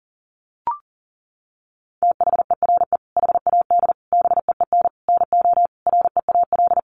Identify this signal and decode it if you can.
Morse code, heard as a keyed tone, about 40 wpm, 700 Hz. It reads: THERE HAD BEEN NO REAL
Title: Sentences from Top 1400 Words 40wpm